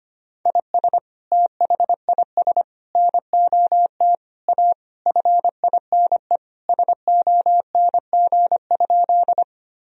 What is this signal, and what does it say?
Morse code, heard as a keyed tone, about 25 wpm, 700 Hz. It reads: IS THIS NOT A FINE SONG?
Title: Sentences from Top 1000 Words - Review Part 2 - 25wpm